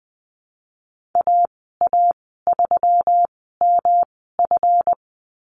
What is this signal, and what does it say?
Morse code, heard as a keyed tone, about 20 wpm, spaced as for 15 wpm, 700 Hz. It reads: AA3MF
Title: 173 Lesson 28g - Number 9 - Call Signs 20wpm